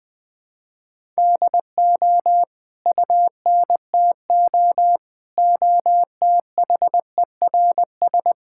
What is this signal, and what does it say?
Morse code, heard as a keyed tone, about 20 wpm, 700 Hz. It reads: DO UNTO OTHERS
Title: Sets of 8 Words - 20wpm